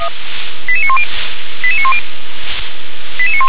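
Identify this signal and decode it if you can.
Five-tone selective call (ZVEI), heard as quick runs of five tones, tones of about 70 ms. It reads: 88910 88910
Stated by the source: ZVEI five-tone selective call received on SDR.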